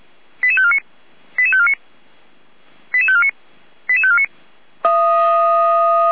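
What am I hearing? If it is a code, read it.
Five-tone selective call (ZVEI), heard as quick runs of five tones, tones of about 70 ms. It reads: 88549 88549 88549 88549
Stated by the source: ZVEI five-tone selective call received on an SDR.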